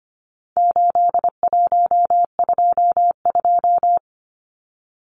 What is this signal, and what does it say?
Morse code, heard as a keyed tone, about 25 wpm, 700 Hz. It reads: 8122
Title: Numbers - 4-Digit 25wpm